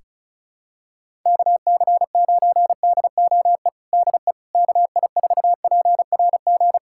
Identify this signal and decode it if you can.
Morse code, heard as a keyed tone, about 34 wpm, 700 Hz. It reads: KC9DOE DE KI4PRG
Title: USA Callsign DE Callsign - 6 Characters - 35wpm